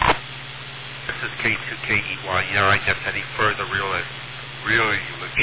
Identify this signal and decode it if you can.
Emergency Alert System (SAME) tones, heard as harsh buzzing bursts, with speech. It is heard from 1.0 to 5.0 s.